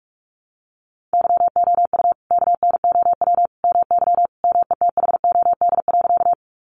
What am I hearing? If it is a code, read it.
Morse code, heard as a keyed tone, about 44 wpm, 700 Hz. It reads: YOU KNOW MY METHOD.
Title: Sentences from Top 600 Words 45wpm